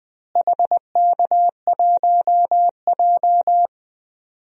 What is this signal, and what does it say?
Morse code, heard as a keyed tone, about 21 wpm, 700 Hz. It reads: HK1J
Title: DXCC-100 Call Signs - 4 Characters - Repeated 2x - 20wpm